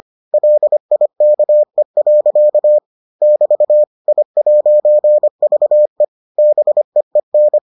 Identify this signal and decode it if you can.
Morse code, heard as a keyed tone, about 25 wpm, 600 Hz. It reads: LIKE. <BT> I'VE BEEN